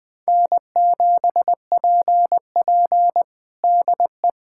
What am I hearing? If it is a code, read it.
Morse code, heard as a keyed tone, about 20 wpm, 700 Hz. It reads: N7PP DE
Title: USA Callsign DE Callsign - 4 Characters - 20wpm